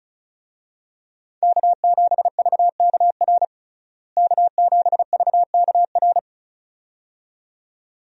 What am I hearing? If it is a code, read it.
Morse code, heard as a keyed tone, about 34 wpm, 700 Hz. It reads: K7VKR K7VKR
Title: USA Call Signs - 5 Characters - Repeated 2x - 35wpm